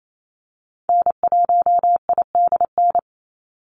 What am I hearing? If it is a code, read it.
Morse code, heard as a keyed tone, about 28 wpm, 700 Hz. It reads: N1IDN
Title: NA SOTA Activator Callsigns 28wpm